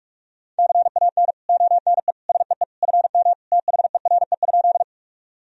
Morse code, heard as a keyed tone, about 42 wpm, 700 Hz. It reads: CAN ONE SEE FM THERE?
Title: Sentences from Top 100 Words and Abbreviations 45wpm